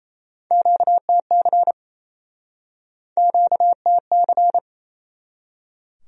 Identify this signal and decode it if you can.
Morse code, heard as a keyed tone, about 38 wpm, 700 Hz. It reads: QTC QTC